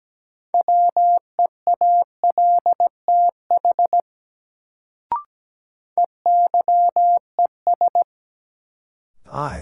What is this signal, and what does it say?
Morse code, heard as a keyed tone, about 17 wpm, 700 Hz. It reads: WEALTH EYES
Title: Sentences from Top 1000 Words - Review Part 4 - New Word Drill - 17wpm